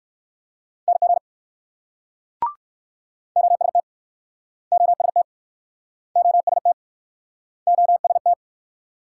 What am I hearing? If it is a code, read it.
Morse code, heard as a keyed tone, about 60 wpm, 700 Hz. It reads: NG OST OST OST OST